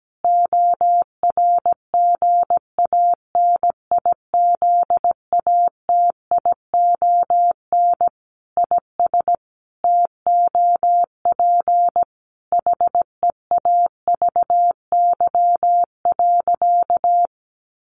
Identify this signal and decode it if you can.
Morse code, heard as a keyed tone, about 17 wpm, 700 Hz. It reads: ORGANIZATION IS TOP HEAVY.
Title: Sentences from Top 1100 Words 17wpm